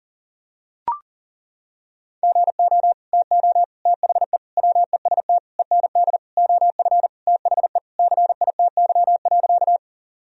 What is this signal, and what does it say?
Morse code, heard as a keyed tone, about 40 wpm, 700 Hz. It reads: GO TO THE WEST END OF THE CITY.
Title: Sentences from Top 1000 Words - Review Part 2 - 40wpm